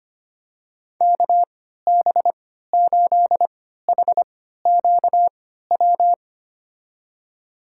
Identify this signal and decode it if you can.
Morse code, heard as a keyed tone, about 25 wpm, spaced as for 14 wpm, 700 Hz. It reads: KB8HQW
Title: USA Call Signs - 6 Characters - 3x Character Spacing - 25wpm